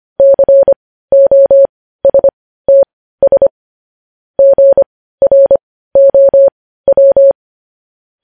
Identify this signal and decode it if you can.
Morse code, heard as a keyed tone, about 25 wpm, spaced as for 15 wpm, 550 Hz. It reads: COSTS GROW
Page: news headlines in morse code at 15 wpm